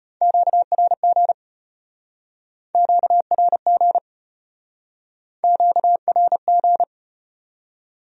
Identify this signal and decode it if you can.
Morse code, heard as a keyed tone, about 33 wpm, 700 Hz. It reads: QRG QRG QRG